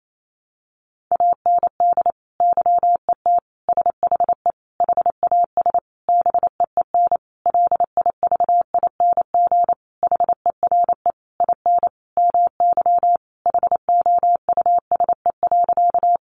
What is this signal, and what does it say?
Morse code, heard as a keyed tone, about 28 wpm, 700 Hz. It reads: AND YET SHE HAS BEEN LIVING HERE IN MY HOUSE.
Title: Sentences from Top 1100 Words 28wpm